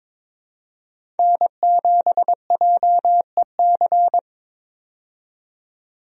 Morse code, heard as a keyed tone, about 22 wpm, 700 Hz. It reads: N7JEC
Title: USA Call Signs - 5 Characters - 22wpm